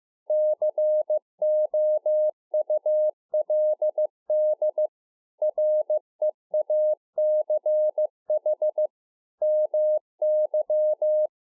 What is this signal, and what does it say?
Morse code, heard as a keyed tone, about 15 wpm, 610 Hz. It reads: COULD REACH MY